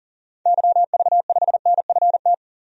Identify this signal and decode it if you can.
Morse code, heard as a keyed tone, about 40 wpm, 700 Hz. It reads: YV5NFT
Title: DXCC-100 Call Signs - 6 Characters - 40wpm